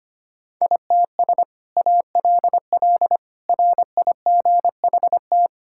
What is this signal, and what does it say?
Morse code, heard as a keyed tone, about 25 wpm, 700 Hz. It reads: ITS ALL RIGHT
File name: practice / 25wpm / Sentences with 2-Word Phrases - 25wpm